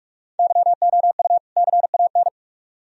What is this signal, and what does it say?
Morse code, heard as a keyed tone, about 44 wpm, 700 Hz. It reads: YOU CAN